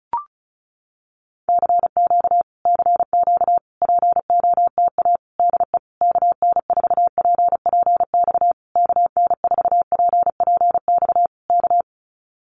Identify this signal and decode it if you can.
Morse code, heard as a keyed tone, about 34 wpm, 700 Hz. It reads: CQ CQ POTA DE KN4PPX KN4PPX K
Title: Common QSO Phrases Encore 35wpm